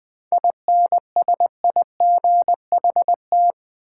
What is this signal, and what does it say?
Morse code, heard as a keyed tone, about 20 wpm, 690 Hz. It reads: INSIGHT
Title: 7 Letter Words - First 3 Letters Given - 20wpm